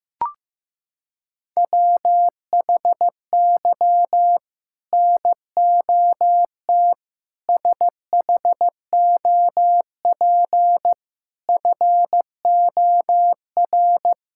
Morse code, heard as a keyed tone, about 15 wpm, 700 Hz. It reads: WHY NOT SHOP FOR
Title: Sentences from Top 1600 Words 15wpm